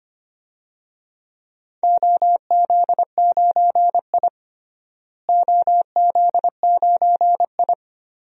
Morse code, heard as a keyed tone, about 25 wpm, 700 Hz. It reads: OZ9I OZ9I
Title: DXCC-100 Call Signs - 4 Characters - Repeated 2x - 25wpm